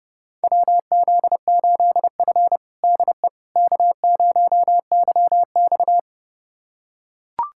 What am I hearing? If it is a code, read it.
Morse code, heard as a keyed tone, about 30 wpm, 700 Hz. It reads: WZ8F DE K0YX
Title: USA Callsign DE Callsign - 4 Characters - 30wpm